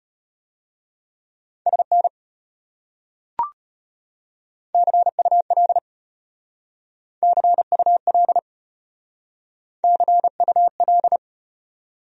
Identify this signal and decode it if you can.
Morse code, heard as a keyed tone, about 33 wpm, 700 Hz. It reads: SN CUL CUL CUL